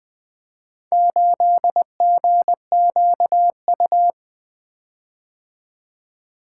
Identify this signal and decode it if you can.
Morse code, heard as a keyed tone, about 23 wpm, 700 Hz. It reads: 8GQU
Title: Speed-Racing - Four Letter-Number Combinations - 20wpm